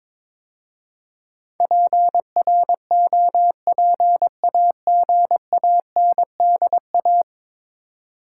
Morse code, heard as a keyed tone, about 22 wpm, 700 Hz. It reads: PROPAGANDA
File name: practice / 22wpm / 10 Letter Words - First 3 Letters Given - 22wpm